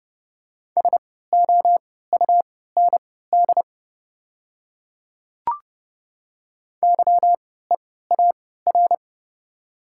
Morse code, heard as a keyed tone, about 30 wpm, spaced as for 17 wpm, 700 Hz. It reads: SOUND YEAR
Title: 3x Character Spacing - Top 200 Words - 30wpm